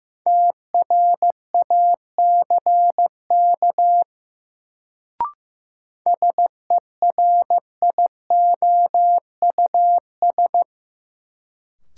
Morse code, heard as a keyed tone, about 15 wpm, 700 Hz. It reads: TRACK SERIOUS
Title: Sentences from Top 1100 Words - New Word Drill Alt - 15wpm